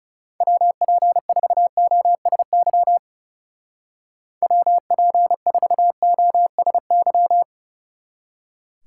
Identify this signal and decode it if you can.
Morse code, heard as a keyed tone, about 38 wpm, 700 Hz. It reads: WP4OSY WP4OSY